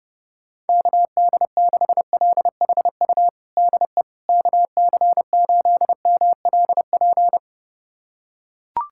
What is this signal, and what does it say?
Morse code, heard as a keyed tone, about 30 wpm, 700 Hz. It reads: KD6LHU DE KC8MLP
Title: USA Callsign DE Callsign 30wpm